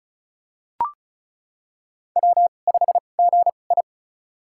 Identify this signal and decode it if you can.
Morse code, heard as a keyed tone, about 35 wpm, spaced as for 25 wpm, 700 Hz. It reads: W5GI
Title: USA Call Signs - 4 Characters - 2x Character Spacing - 35wpm